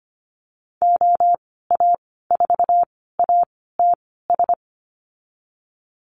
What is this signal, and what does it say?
Morse code, heard as a keyed tone, about 25 wpm, spaced as for 16 wpm, 700 Hz. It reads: OA4ATS
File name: Lesson 15g - Number 5 - Call Signs 25wpm